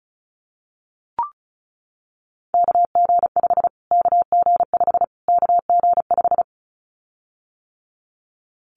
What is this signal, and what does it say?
Morse code, heard as a keyed tone, about 34 wpm, 700 Hz. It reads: KG5 KG5 KG5
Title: Callsign Prefixes for Top 25 DXCC - Repeated 3x - 35wpm